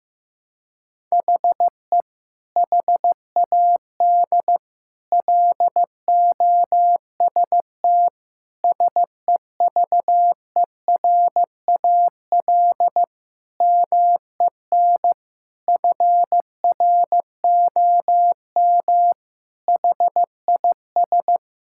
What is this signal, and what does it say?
Morse code, heard as a keyed tone, about 15 wpm, 700 Hz. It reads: HE HAD LOST SEVERAL MEN FROM HIS
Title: Sentences from Top 1600 Words 15wpm